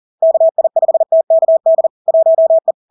Morse code, heard as a keyed tone, about 40 wpm, 650 Hz. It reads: KI5TKD 1E
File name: LARK-FD-2022-Log-in-CW-40wpm-650-Hz